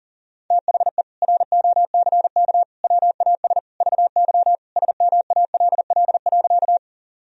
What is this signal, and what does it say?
Morse code, heard as a keyed tone, about 40 wpm, 700 Hz. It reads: THE ROCK WAS VY SMALL.
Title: Sentences from Top 400 Words and Abbreviations 40wpm